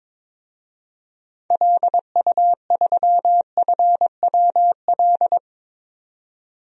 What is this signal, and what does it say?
Morse code, heard as a keyed tone, about 22 wpm, 700 Hz. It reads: LU3FWL
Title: DXCC-100 Call Signs - 6 Characters - 22wpm